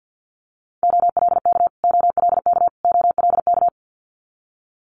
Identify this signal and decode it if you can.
Morse code, heard as a keyed tone, about 50 wpm, 700 Hz. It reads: OLK OLK OLK
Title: Multi-letter Phonemes Repeated 3x 50wpm